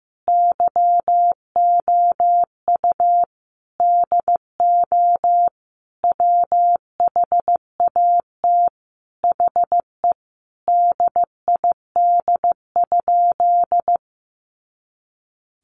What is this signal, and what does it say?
Morse code, heard as a keyed tone, about 15 wpm, 700 Hz. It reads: YOU DO WHAT HE DID?
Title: Sentences from Top 100 Words 15wpm